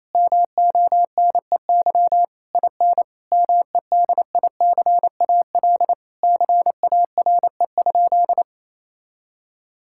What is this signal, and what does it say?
Morse code, heard as a keyed tone, about 28 wpm, 700 Hz. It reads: MONEY IN MEDICAL CARE?
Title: Sentences from Top 1000 Words - Review Part 5 - 28wpm